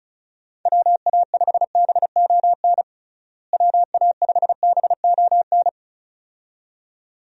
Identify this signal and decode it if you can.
Morse code, heard as a keyed tone, about 36 wpm, 700 Hz. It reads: WA5BON WA5BON
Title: USA Call Signs - 6 Characters - Repeated 2x - 35wpm